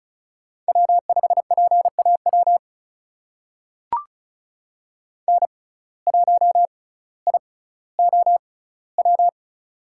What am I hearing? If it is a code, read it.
Morse code, heard as a keyed tone, about 34 wpm, spaced as for 18 wpm, 700 Hz. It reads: W5PAW N1IOW
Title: USA Call Signs - 5 Characters - 6x Character Spacing - 35wpm